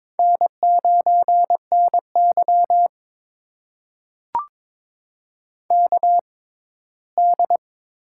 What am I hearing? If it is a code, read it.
Morse code, heard as a keyed tone, about 22 wpm, spaced as for 15 wpm, 700 Hz. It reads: N9NY KD
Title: USA Call Signs - 4 Characters - 6x Character Spacing - 22wpm